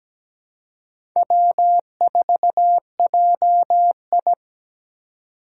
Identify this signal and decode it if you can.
Morse code, heard as a keyed tone, about 17 wpm, 700 Hz. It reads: W4JI
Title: USA Call Signs - 4 Characters - 2x Character Spacing - 17wpm